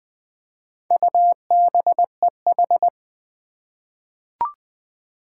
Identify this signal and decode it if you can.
Morse code, heard as a keyed tone, about 20 wpm, 700 Hz. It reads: UBEH
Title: Four Challenging Letter-Number Combinations 20wpm